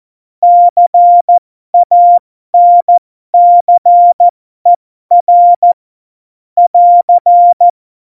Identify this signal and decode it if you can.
Morse code, heard as a keyed tone, about 14 wpm, spaced as for 12 wpm, 700 Hz. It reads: CANCER <AR>
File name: Morsekurs / Witze